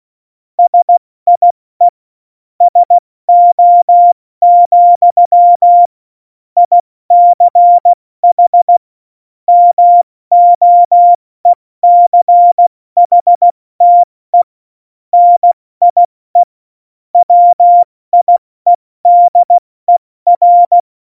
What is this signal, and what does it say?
Morse code, heard as a keyed tone, about 16 wpm, 700 Hz. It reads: SIE SO, ICH MOECHTE NIE WIEDER
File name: Morsekurs / Witze / Witz 08. September 2020, 70 F80